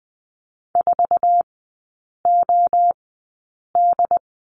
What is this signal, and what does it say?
Morse code, heard as a keyed tone, about 20 wpm, spaced as for 8 wpm, 700 Hz. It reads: 4OD
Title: 178 Lesson 29e - Slash - Three Characters 20wpm